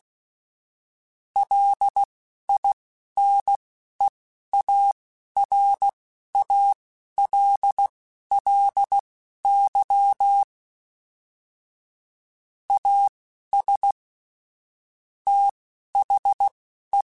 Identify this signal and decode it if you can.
Morse code, heard as a keyed tone, about 16 wpm, spaced as for 11 wpm, 800 Hz. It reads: LINEARALLY AS THE
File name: Technical Subjects / Technical Subjects-mp3s / XL @10